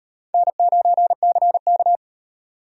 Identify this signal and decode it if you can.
Morse code, heard as a keyed tone, about 35 wpm, 700 Hz. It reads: N9CK